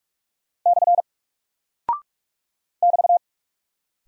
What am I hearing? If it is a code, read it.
Morse code, heard as a keyed tone, about 47 wpm, 700 Hz. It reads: / <BT>
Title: Single Character (Letter:Number:Punctuation:Prosign) Alt 45wpm